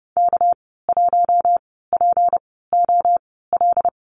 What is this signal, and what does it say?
Morse code, heard as a keyed tone, about 30 wpm, spaced as for 17 wpm, 700 Hz. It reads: K1POL
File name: Lesson 37g - Letter x - Call Signs 30wpm